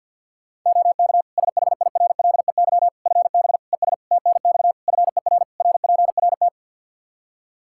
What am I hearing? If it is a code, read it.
Morse code, heard as a keyed tone, about 51 wpm, 700 Hz. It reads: OK SHIRLEY FB ES TNX FER RPRT
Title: Practice QSO - Exchange 2 - Info - 50wpm